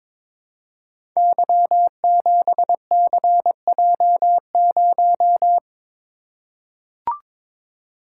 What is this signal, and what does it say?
Morse code, heard as a keyed tone, about 22 wpm, 700 Hz. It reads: Y7CJ0
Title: Code Groups - 5 Letter:Numbers - 22wpm